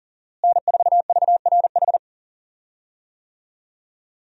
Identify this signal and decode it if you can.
Morse code, heard as a keyed tone, about 40 wpm, 700 Hz. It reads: N4VRH
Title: USA Call Signs - 5 Characters - Repeated 2x - 40wpm